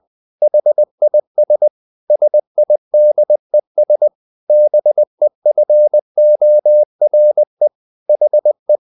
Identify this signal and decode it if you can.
Morse code, heard as a keyed tone, about 20 wpm, 600 Hz. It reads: HIS SIDES BEFORE HE
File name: the old man and the sea / the old man and the sea part 1 20wpm